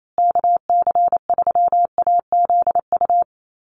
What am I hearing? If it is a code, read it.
Morse code, heard as a keyed tone, about 28 wpm, 700 Hz. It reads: KC3AZU
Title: USA Call Signs Repeated 2x 28wpm